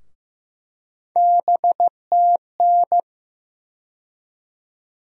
Morse code, heard as a keyed tone, about 15 wpm, 700 Hz. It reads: BTN
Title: USA Call Sign Postfixes - 3 Letters 15wpm